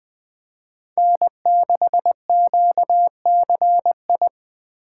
Morse code, heard as a keyed tone, about 21 wpm, 700 Hz. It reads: N6QCI
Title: USA Call Signs - 5 Characters - 3x Character Spacing - 20wpm